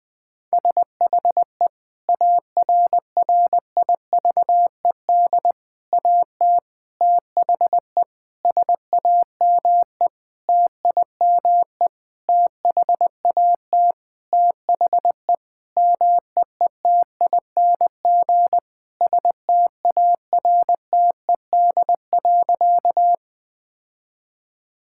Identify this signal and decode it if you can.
Morse code, heard as a keyed tone, about 20 wpm, 700 Hz. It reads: SHE ARRIVED AT THE SAME TIME THAT THE MEETING STARTED.
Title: Sentences with 5-Word Phrases - 20wpm